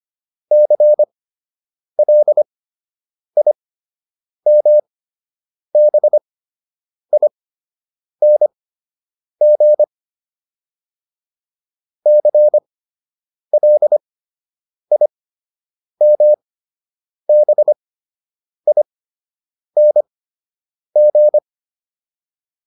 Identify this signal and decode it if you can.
Morse code, heard as a keyed tone, about 25 wpm, spaced as for 8 wpm, 600 Hz. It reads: CLIMBING CLIMBING
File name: ING8 08